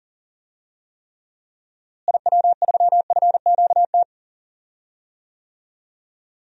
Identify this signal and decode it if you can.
Morse code, heard as a keyed tone, about 40 wpm, 700 Hz. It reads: IW3FQT
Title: DXCC-100 Call Signs - 6 Characters - Repeated 3x - 40wpm